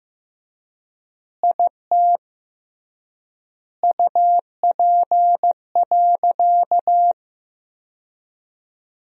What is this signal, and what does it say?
Morse code, heard as a keyed tone, about 15 wpm, 700 Hz. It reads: IT UP.
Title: 3x Word-Spacing - Sentences from Top 100 Words - 15wpm